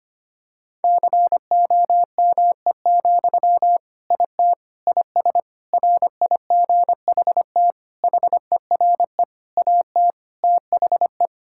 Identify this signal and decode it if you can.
Morse code, heard as a keyed tone, about 25 wpm, 700 Hz. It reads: COME, IT IS RIGHT HERE AT THE
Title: Sentences from Top 1600 Words 25wpm